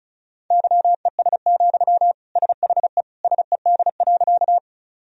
Morse code, heard as a keyed tone, about 36 wpm, 700 Hz. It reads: YES, SHE SED.
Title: Sentences from Top 500 Words and Abbreviations 35wpm